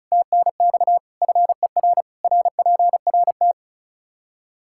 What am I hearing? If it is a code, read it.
Morse code, heard as a keyed tone, about 36 wpm, 700 Hz. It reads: TNX FER RPRT
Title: K1USN SST QSO Practice 35wpm